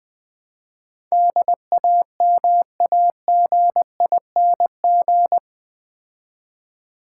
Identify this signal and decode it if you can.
Morse code, heard as a keyed tone, about 20 wpm, 700 Hz. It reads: DAMAGING
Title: Build A Word - 2 Letters At A Time - 8-Letter Words - 20wpm